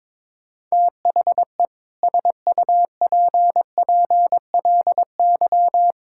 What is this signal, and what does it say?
Morse code, heard as a keyed tone, about 22 wpm, 700 Hz. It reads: THE SUPPLY
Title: Sentences from Top 1000 Words - Review Part 6 - 22wpm